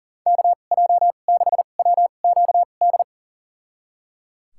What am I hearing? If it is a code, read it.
Morse code, heard as a keyed tone, about 37 wpm, spaced as for 28 wpm, 700 Hz. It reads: KJ6WQD
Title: USA Call Signs - 6 Characters - 2x Character Spacing - 40wpm